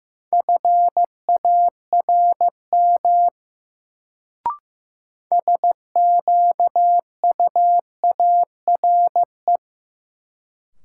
Morse code, heard as a keyed tone, about 15 wpm, 700 Hz. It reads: FARM SQUARE
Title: Sentences from Top 1200 Words - New Word Drill Alt - 15wpm